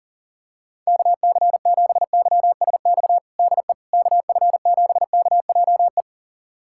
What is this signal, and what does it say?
Morse code, heard as a keyed tone, about 40 wpm, 700 Hz. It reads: KC7YSX DE KF7KJE
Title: USA Callsign DE Callsign - 6 Characters - 40wpm